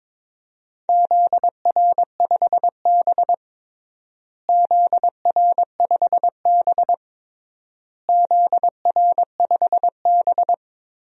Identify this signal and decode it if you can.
Morse code, heard as a keyed tone, about 22 wpm, 700 Hz. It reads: ZR5B ZR5B ZR5B
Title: DXCC-100 Call Signs - 4 Characters - Repeated 3x - 22wpm